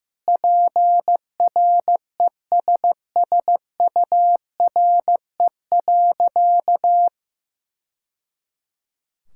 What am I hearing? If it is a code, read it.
Morse code, heard as a keyed tone, about 15 wpm, 700 Hz. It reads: PRESSURE.
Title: Sentences from Top 1000 Words - Review Part 6 - 15wpm